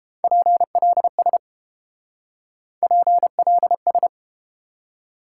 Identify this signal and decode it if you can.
Morse code, heard as a keyed tone, about 30 wpm, 700 Hz. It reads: PLS PLS